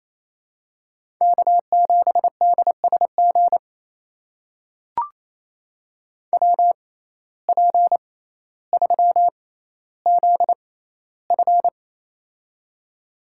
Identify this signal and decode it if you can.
Morse code, heard as a keyed tone, about 28 wpm, spaced as for 14 wpm, 700 Hz. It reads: K7DSG WP3ZF
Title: USA Call Signs - 5 Characters - 6x Character Spacing - 28wpm